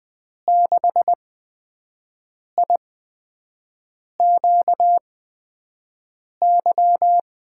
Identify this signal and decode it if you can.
Morse code, heard as a keyed tone, about 20 wpm, spaced as for 5 wpm, 700 Hz. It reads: 6IQY
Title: USA Call Signs - 6 Characters - 8x Character Spacing - 20wpm